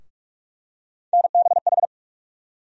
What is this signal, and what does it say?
Morse code, heard as a keyed tone, about 48 wpm, 700 Hz. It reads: NBH
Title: USA Call Sign Postfixes - 3 Letters 45wpm